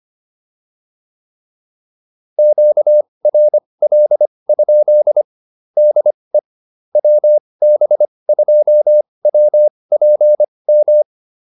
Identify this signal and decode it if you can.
Morse code, heard as a keyed tone, about 25 wpm, spaced as for 20 wpm, 600 Hz. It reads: QRL? DE WB2WPM